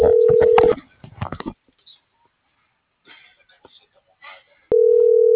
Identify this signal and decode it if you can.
Ringback tone (North America), 440 + 480 Hz.
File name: CENTRAL BLN / out hector.tellez